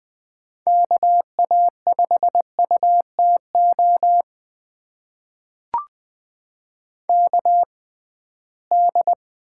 Morse code, heard as a keyed tone, about 20 wpm, spaced as for 15 wpm, 700 Hz. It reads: KA5UTO KD
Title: USA Call Signs - 6 Characters - 6x Character Spacing - 20wpm